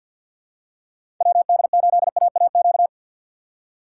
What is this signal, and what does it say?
Morse code, heard as a keyed tone, about 52 wpm, 690 Hz. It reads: WD8AAQ
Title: USA Call Signs - 2x Character Spacing - 50wpm